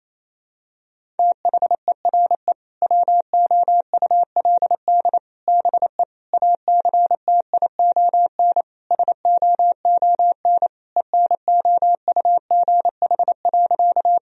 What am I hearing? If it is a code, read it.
Morse code, heard as a keyed tone, about 28 wpm, 700 Hz. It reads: THERE WOULD BE ACTION SOON ENOUGH.
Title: Sentences from Top 1000 Words - Review Part 4 - 28wpm